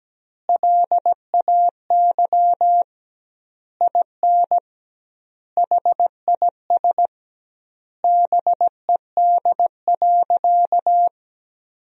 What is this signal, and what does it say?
Morse code, heard as a keyed tone, about 17 wpm, 700 Hz. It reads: LAY IN HIS BED.
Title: 2x Word-Spacing - Sentences from Top 500 Words - 17wpm